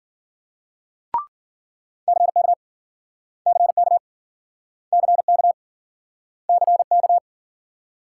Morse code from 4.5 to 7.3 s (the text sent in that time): CK CK